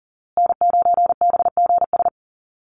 Morse code, heard as a keyed tone, about 40 wpm, 700 Hz. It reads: N9BGS
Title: USA Call Signs Repeated 3x 40wpm